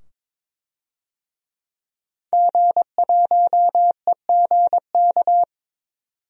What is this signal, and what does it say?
Morse code, heard as a keyed tone, about 22 wpm, 700 Hz. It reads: G1EGK
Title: DXCC-100 Call Signs - 5 Characters - 22wpm